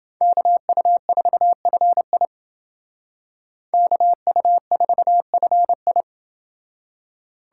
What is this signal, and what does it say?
Morse code, heard as a keyed tone, about 28 wpm, 700 Hz. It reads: KU4FI KU4FI